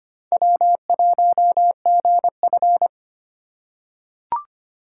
Morse code, heard as a keyed tone, about 25 wpm, 700 Hz. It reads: W1GF
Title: USA Call Signs - 3x Character Spacing - 25wpm